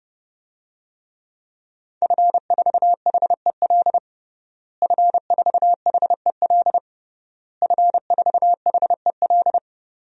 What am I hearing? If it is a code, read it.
Morse code, heard as a keyed tone, about 32 wpm, 700 Hz. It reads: F4HEL F4HEL F4HEL
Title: DXCC-100 Call Signs - 5 Characters - Repeated 3x - 30wpm